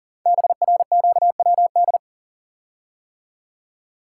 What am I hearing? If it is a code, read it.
Morse code, heard as a keyed tone, about 38 wpm, 700 Hz. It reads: BRQWD
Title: Code Groups - 5 Letters - 40wpm